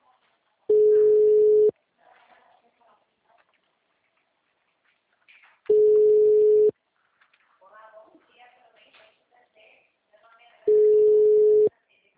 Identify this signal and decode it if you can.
Ringback tone (Europe, 425 Hz), with speech.